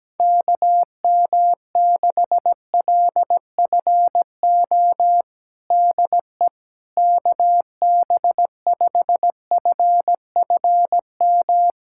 Morse code, heard as a keyed tone, about 17 wpm, 700 Hz. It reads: KM6LFO DE KB5FFM
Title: USA Callsign DE Callsign - 6 Characters - 17wpm